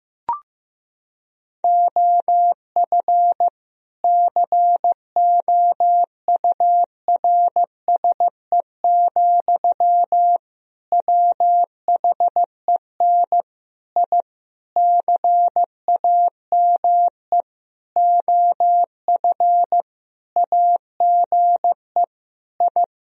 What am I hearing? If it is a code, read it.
Morse code, heard as a keyed tone, about 15 wpm, 700 Hz. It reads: OF COURSE, WHEN I CAME OF AGE I
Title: Sentences from Top 1000 Words - Review Part 3 - 15wpm